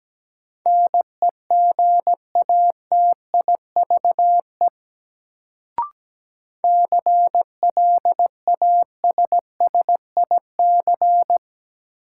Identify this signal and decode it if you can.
Morse code, heard as a keyed tone, about 17 wpm, 700 Hz. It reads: NEGATIVE CLASSIC
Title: Sentences from Top 1000 Words - New Word Drill - 17wpm